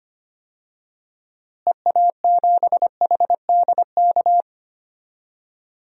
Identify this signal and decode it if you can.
Morse code, heard as a keyed tone, about 25 wpm, 700 Hz. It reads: EA7HDK
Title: DXCC-100 Call Signs - 6 Characters - Repeated 3x - 25wpm